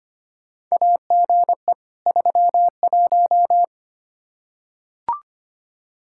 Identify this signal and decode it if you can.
Morse code, heard as a keyed tone, about 25 wpm, 700 Hz. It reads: AGE 31
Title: Practice QSO - Exchange 3 - More Info - 25wpm